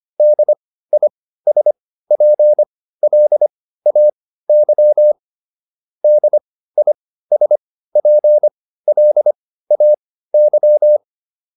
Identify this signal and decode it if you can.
Morse code, heard as a keyed tone, about 25 wpm, spaced as for 15 wpm, 600 Hz. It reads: DISPLAY DISPLAY